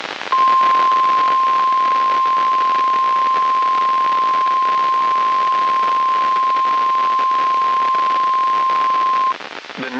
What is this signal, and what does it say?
Emergency Alert System (SAME) tones (the 1050 Hz attention tone), with noise. It is heard from 0.3 to 9.4 s.
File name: eas messages / EAS WXM95 Towanda